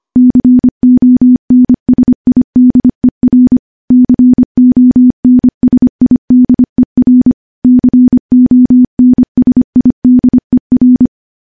Morse code, heard as a keyed tone, about 25 wpm, 250 Hz. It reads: CONSIDER CONSIDER CONSIDER
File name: morse / morse-training-25WPM